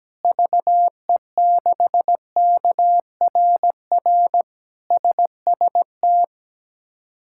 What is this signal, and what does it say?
Morse code, heard as a keyed tone, about 17 wpm, 700 Hz. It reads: VE6KRR SST
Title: K1USN SST QSO Practice - 17wpm